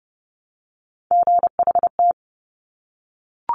Morse code, heard as a keyed tone, about 30 wpm, 700 Hz. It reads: GHT
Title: Top Three Letter Combinations in Words 30wpm